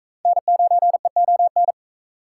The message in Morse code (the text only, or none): N9EON